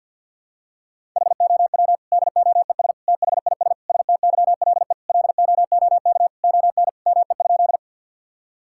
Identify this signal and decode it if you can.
Morse code, heard as a keyed tone, about 50 wpm, 700 Hz. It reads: HOW DOES THIS STYLE LOOK ON ME?
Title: Sentences from Top 1000 Words - Review Part 5 - 50wpm